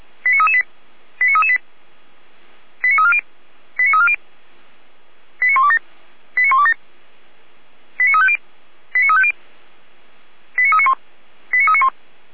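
Five-tone selective call (ZVEI), heard as quick runs of five tones, tones of about 70 ms. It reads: 89338 89338 89340 89340 89127 89127 89350 89350 89391 89391
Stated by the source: ZVEI five-tone call received by SDR, recorded by the Orlamünde fire brigade (feuerwehr-orlamuende).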